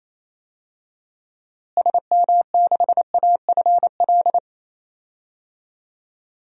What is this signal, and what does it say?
Morse code, heard as a keyed tone, about 28 wpm, 700 Hz. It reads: SM6AFL
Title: DXCC-100 Call Signs - 6 Characters - Repeated 3x - 28wpm